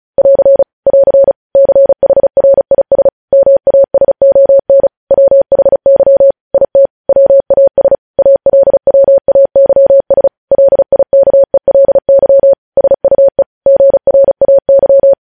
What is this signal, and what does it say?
Morse code, heard as a keyed tone, about 36 wpm, 550 Hz. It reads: <AR> <AR> CHRIS MASON WHY IT WAS ALWAYS LIKELY SUE GRAY